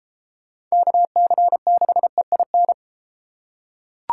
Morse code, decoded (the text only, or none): KC6EIN